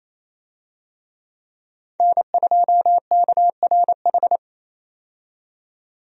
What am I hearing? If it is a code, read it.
Morse code, heard as a keyed tone, about 27 wpm, 700 Hz. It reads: N2KRH
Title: USA Call Signs - 5 Characters - 28wpm